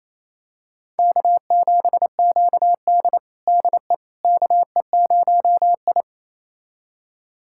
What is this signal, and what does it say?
Morse code, heard as a keyed tone, about 27 wpm, 700 Hz. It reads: K7QD DE KE0I
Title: USA Callsign DE Callsign - 4 Characters - 28wpm